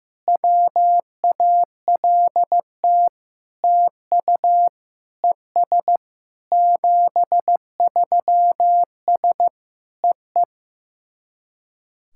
Morse code, heard as a keyed tone, about 16 wpm, 700 Hz. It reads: WALT TU ES 73S EE
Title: SOTA QSO Practice 15wpm